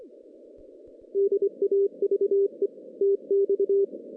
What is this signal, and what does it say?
Morse code, heard as a keyed tone, about 25 wpm, 400 Hz. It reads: DAVE TX
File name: 2016 JAN CW NAQP / 2016 Jan naqp k5ot - 0-10